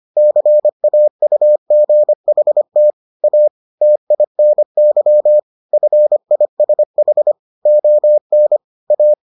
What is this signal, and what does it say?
Morse code, heard as a keyed tone, about 25 wpm, 600 Hz. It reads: CAUGHT A TINY FISH ON A